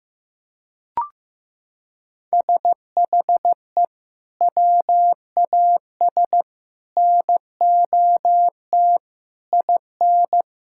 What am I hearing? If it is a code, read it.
Morse code, heard as a keyed tone, about 15 wpm, 700 Hz. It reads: SHE WAS NOT IN
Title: Sentences from Top 1600 Words 15wpm